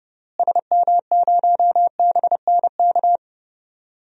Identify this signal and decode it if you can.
Morse code, heard as a keyed tone, about 32 wpm, 700 Hz. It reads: SM0BNK
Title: DXCC-100 Call Signs - 6 Characters - Repeated 3x - 30wpm